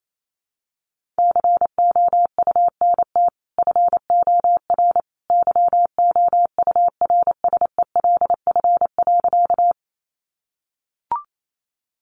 Morse code, heard as a keyed tone, about 28 wpm, 700 Hz. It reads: COUNT FOR YOURSELF.
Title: Sentences from Top 1500 Words 28wpm